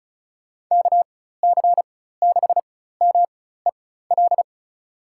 Morse code, heard as a keyed tone, about 34 wpm, spaced as for 16 wpm, 690 Hz. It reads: KC6MEL
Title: USA Call Signs - 4x Character Spacing - 35wpm